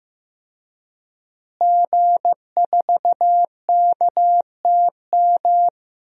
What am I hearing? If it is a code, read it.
Morse code, heard as a keyed tone, about 15 wpm, 700 Hz. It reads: G4KTM
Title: DXCC-100 Call Signs - 5 Characters - Repeated 3x - 15wpm